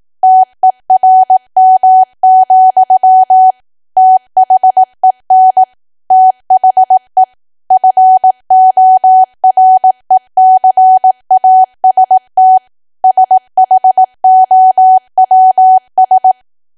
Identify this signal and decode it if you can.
Morse code, heard as a keyed tone, about 18 wpm, 750 Hz. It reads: TERM, THEN THE FORECAST SHOWS
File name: cwa / W1AW-Level2 / 18WPM